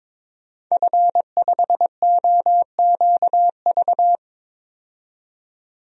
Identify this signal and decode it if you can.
Morse code, heard as a keyed tone, about 22 wpm, 700 Hz. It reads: F5OQV
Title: DXCC-100 Call Signs - 5 Characters - 22wpm